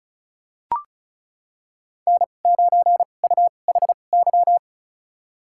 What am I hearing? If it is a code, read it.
Morse code, heard as a keyed tone, about 34 wpm, spaced as for 25 wpm, 700 Hz. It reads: N9UHY
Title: USA Call Signs - 5 Characters - 2x Character Spacing - 35wpm